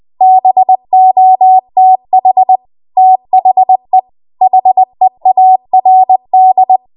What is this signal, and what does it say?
Morse code, heard as a keyed tone, about 20 wpm, 750 Hz. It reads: BOTH THE HEARD
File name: Morse / 20WPM